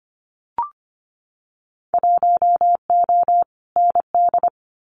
Morse code, heard as a keyed tone, about 25 wpm, 700 Hz. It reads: 1O ND
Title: Winter Field Day Exchanges 25wpm